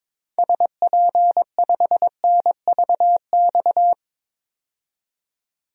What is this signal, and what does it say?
Morse code, heard as a keyed tone, about 22 wpm, 700 Hz. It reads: SP5NVX
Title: DXCC-100 Call Signs - 6 Characters - 22wpm